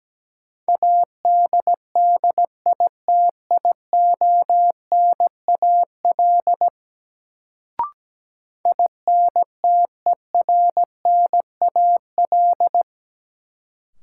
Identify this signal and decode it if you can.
Morse code, heard as a keyed tone, about 17 wpm, 700 Hz. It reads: ADDITIONAL INTERNAL
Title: Sentences from Top 1400 Words - New Word Drill Alt - 17wpm